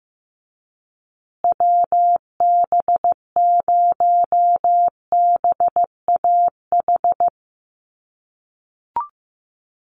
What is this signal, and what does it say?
Morse code, heard as a keyed tone, about 15 wpm, 700 Hz. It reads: WB0BAH
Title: USA Call Signs Repeated 3x 15wpm